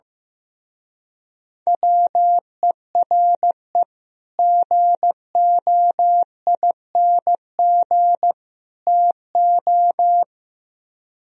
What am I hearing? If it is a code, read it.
Morse code, heard as a keyed tone, about 16 wpm, 700 Hz. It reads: WERE GOING TO
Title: Speed-Racing - Sets of 3 Words 15wpm